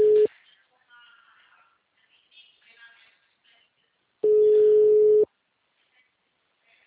Ringback tone (Europe, 425 Hz).
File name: out- -2001- - - .2